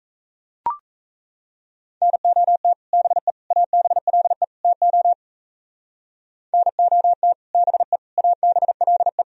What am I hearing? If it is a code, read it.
Morse code, heard as a keyed tone, about 33 wpm, 700 Hz. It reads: NOT BE ABLE TO NOT BE ABLE
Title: Speed-Racing - Sets of 4 Words 28wpm